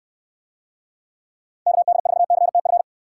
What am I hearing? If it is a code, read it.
Morse code, heard as a keyed tone, about 69 wpm, 700 Hz. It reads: KD4CTF